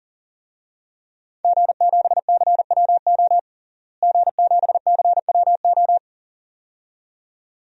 Morse code, heard as a keyed tone, about 40 wpm, 700 Hz. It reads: G7CWO G7CWO
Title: DXCC-100 Call Signs - 5 Characters - Repeated 2x - 40wpm